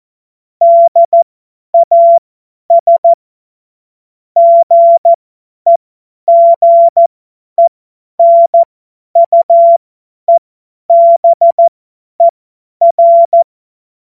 Morse code, heard as a keyed tone, about 14 wpm, spaced as for 10 wpm, 680 Hz. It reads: DAS GEGENUEBER
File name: Morsekurs / Witze / Witz 25. Oktober 2022 50 F70